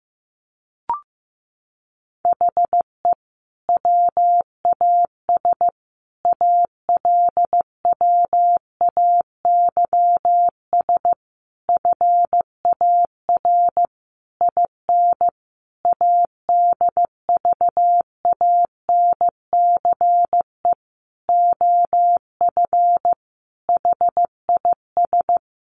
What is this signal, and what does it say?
Morse code, heard as a keyed tone, about 15 wpm, 700 Hz. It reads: HE WAS ALWAYS FAR IN ADVANCE OF HIS
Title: Sentences from Top 1400 Words 15wpm